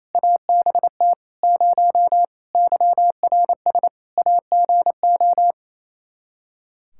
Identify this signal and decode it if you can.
Morse code, heard as a keyed tone, about 28 wpm, 700 Hz. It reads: ABT 0 YRS AGO
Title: Practice QSO - Exchange 3 - More Info - 28wpm